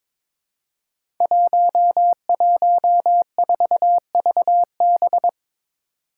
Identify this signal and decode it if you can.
Morse code, heard as a keyed tone, about 23 wpm, 700 Hz. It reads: 114VB
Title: Five Challenging Letter-Number Combinations 22wpm